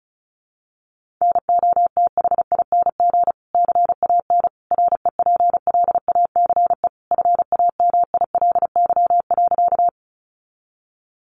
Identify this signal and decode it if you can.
Morse code, heard as a keyed tone, about 34 wpm, 700 Hz. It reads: NOTHING CAN REPLACE FAMILY.
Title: Sentences from Top 1400 Words 35wpm